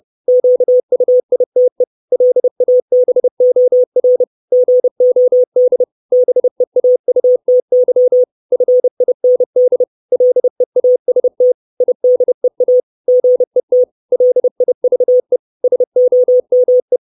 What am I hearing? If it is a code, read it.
Morse code, heard as a keyed tone, about 30 wpm, 500 Hz. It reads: QUITE LABOR GOD BEAUTY FIND LEAST IDEA GET LIVE SOME
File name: 632 Words 30WPM